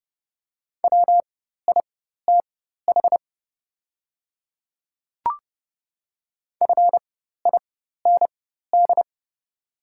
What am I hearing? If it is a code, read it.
Morse code, heard as a keyed tone, about 30 wpm, spaced as for 14 wpm, 700 Hz. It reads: WITH FIND
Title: 4x Character Spacing - Top 100 Words - 30wpm